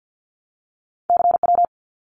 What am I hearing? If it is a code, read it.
Morse code, heard as a keyed tone, about 52 wpm, 700 Hz. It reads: CW
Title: Abbreviations Repeated 3x - 50wpm